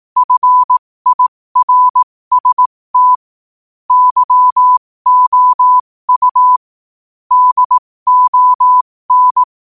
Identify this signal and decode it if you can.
Morse code, heard as a keyed tone, about 18 wpm, spaced as for 15 wpm, 1000 Hz. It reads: FIRST YOU DON
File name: qotdfiles / 15wpm